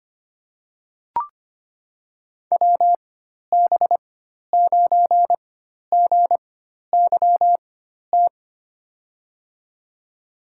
Morse code, heard as a keyed tone, about 25 wpm, spaced as for 12 wpm, 700 Hz. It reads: WB9GYT
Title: USA Call Signs - 4x Character Spacing - 25wpm